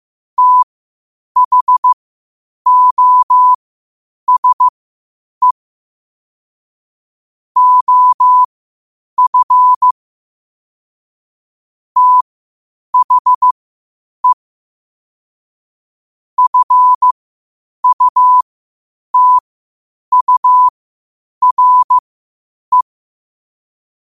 Morse code, heard as a keyed tone, about 15 wpm, spaced as for 8 wpm, 1000 Hz. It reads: THOSE OF THE FUTURE
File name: qotdfiles / 8wpm